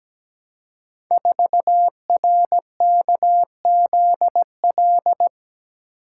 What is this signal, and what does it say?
Morse code, heard as a keyed tone, about 17 wpm, 700 Hz. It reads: 4RKZL
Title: Code Groups - 5 Letter:Numbers - 17wpm